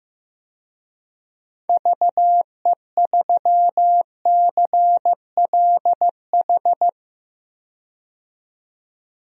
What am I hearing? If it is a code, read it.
Morse code, heard as a keyed tone, about 15 wpm, 700 Hz. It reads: VE3CLH
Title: DXCC-100 Call Signs - 6 Characters - 15wpm